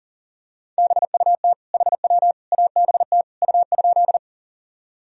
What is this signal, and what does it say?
Morse code, heard as a keyed tone, about 40 wpm, 700 Hz. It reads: BUT HW ABT U?
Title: Sentences from Top 100 Words and Abbreviations 40wpm